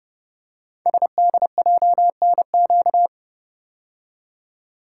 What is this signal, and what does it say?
Morse code, heard as a keyed tone, about 30 wpm, 700 Hz. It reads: SDJNQ
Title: Code Groups - 5 Letter:Numbers - 30wpm